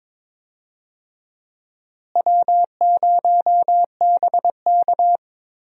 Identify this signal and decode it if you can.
Morse code, heard as a keyed tone, about 22 wpm, 700 Hz. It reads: W0BK
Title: USA Call Signs - 4 Characters - Repeated 2x - 22wpm